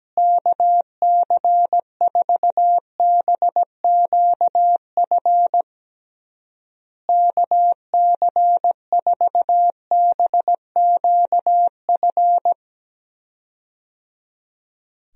Morse code, heard as a keyed tone, about 17 wpm, 700 Hz. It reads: KC4BQF KC4BQF
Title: USA Call Signs - 6 Characters - Repeated 3x - 17wpm